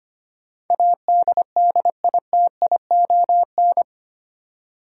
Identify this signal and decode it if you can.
Morse code, heard as a keyed tone, about 25 wpm, 700 Hz. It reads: ADDITION
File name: Eight Letter Words - 25wpm